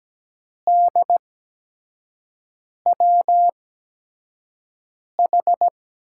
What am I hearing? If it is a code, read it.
Morse code, heard as a keyed tone, about 17 wpm, spaced as for 5 wpm, 700 Hz. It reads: DWH
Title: USA Call Signs - 5 Characters - 8x Character Spacing - 17wpm